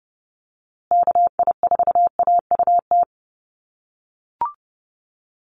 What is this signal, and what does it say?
Morse code, heard as a keyed tone, about 30 wpm, 700 Hz. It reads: KI4AUT
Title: USA Call Signs Repeated 2x 30wpm